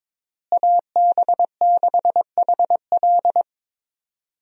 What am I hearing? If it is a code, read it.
Morse code, heard as a keyed tone, about 22 wpm, 700 Hz. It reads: AB6HL
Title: USA Call Signs - 5 Characters - Repeated 3x - 22wpm